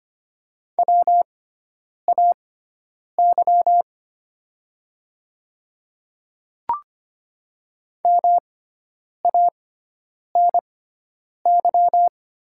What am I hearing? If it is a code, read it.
Morse code, heard as a keyed tone, about 25 wpm, spaced as for 9 wpm, 700 Hz. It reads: WAY MANY
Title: 6x Character Spacing - Top 100 Words - 25wpm